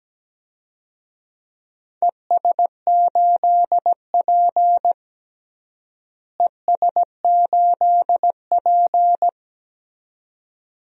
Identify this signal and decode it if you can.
Morse code, heard as a keyed tone, about 17 wpm, 700 Hz. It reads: ES8P ES8P
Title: DXCC-100 Call Signs - 4 Characters - Repeated 2x - 17wpm